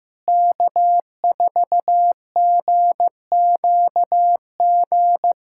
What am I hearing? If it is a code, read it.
Morse code, heard as a keyed tone, about 15 wpm, 700 Hz. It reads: K4GQG
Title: USA Callsign DE Callsign - 5 Characters - 15wpm